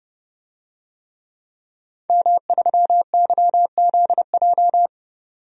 Morse code, heard as a keyed tone, about 29 wpm, 700 Hz. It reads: M3YZJ
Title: DXCC-100 Call Signs - 5 Characters - 30wpm